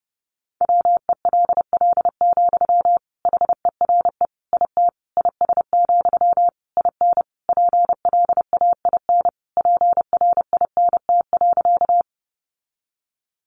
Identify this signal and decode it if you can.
Morse code, heard as a keyed tone, about 30 wpm, 700 Hz. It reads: WELL, HERE IT IS, IN PLAIN PRINT.
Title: Sentences from Top 1400 Words 30wpm